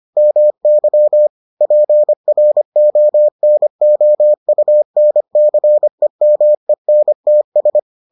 Morse code, heard as a keyed tone, about 25 wpm, 600 Hz. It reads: MY PRONOUNCEMENTS